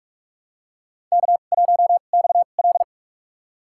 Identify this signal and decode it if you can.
Morse code, heard as a keyed tone, about 42 wpm, spaced as for 31 wpm, 700 Hz. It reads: K1XL
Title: USA Call Signs - 4 Characters - 2x Character Spacing - 45wpm